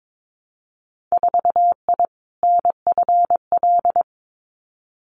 Morse code, heard as a keyed tone, about 22 wpm, 700 Hz. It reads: 4I NFL
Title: Winter Field Day Exchanges Encore 22wpm